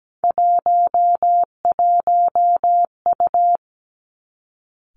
Morse code, heard as a keyed tone, about 17 wpm, 700 Hz. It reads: 11U
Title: Three Challenging Letter-Number Combinations 17wpm